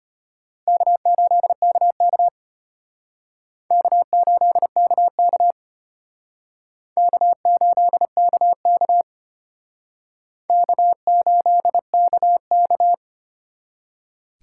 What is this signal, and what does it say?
Morse code, heard as a keyed tone, about 33 wpm, 700 Hz. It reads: K8KK K8KK K8KK K8KK